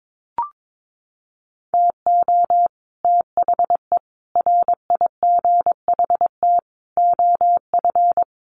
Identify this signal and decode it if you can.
Morse code, heard as a keyed tone, about 22 wpm, 700 Hz. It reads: TO THE RIGHT OF
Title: Sets of 5 Words Encore 22wpm